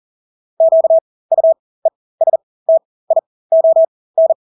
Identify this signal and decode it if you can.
Morse code, heard as a keyed tone, about 40 wpm, spaced as for 20 wpm, 650 Hz. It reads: QUESTION
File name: Engelske ord 40-20 wpm